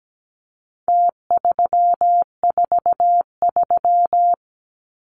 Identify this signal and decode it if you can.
Morse code, heard as a keyed tone, about 17 wpm, 700 Hz. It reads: T343
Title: Numbers - Contest Style - 4-Digit 17wpm